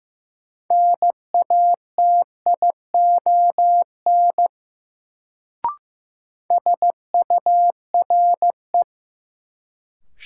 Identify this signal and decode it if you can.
Morse code, heard as a keyed tone, about 15 wpm, 700 Hz. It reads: NATION SURE
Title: Sets of 3 Words - New Word Drill - 15wpm